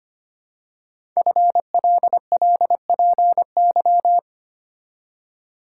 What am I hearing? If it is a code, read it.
Morse code, heard as a keyed tone, about 25 wpm, 700 Hz. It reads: FLLPY
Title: Code Groups - 5 Letters - 25wpm